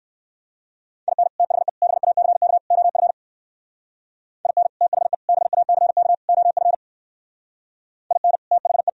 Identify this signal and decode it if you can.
Morse code, heard as a keyed tone, about 64 wpm, 700 Hz. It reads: IN THE BACK OF IN THE BACK OF IN THE